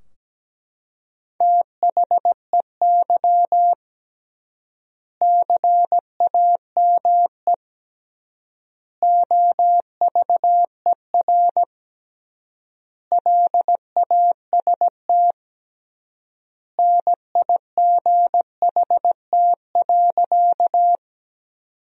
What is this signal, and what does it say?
Morse code, heard as a keyed tone, about 17 wpm, 700 Hz. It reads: THEY CAME OVER LAST NIGHT.
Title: 3x Word-Spacing - Sentences from Top 300 Words - 17wpm